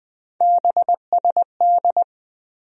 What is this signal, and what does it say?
Morse code, heard as a keyed tone, about 20 wpm, 700 Hz. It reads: BSD
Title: USA Call Sign Postfixes - 3 Letters 20wpm